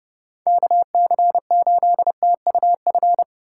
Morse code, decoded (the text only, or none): KC8TUF